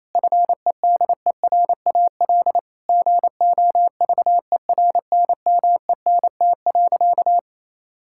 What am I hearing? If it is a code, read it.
Morse code, heard as a keyed tone, about 27 wpm, 700 Hz. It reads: FEDERAL GOVERNMENT.
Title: Sentences from Top 1000 Words - Review Part 5 - 28wpm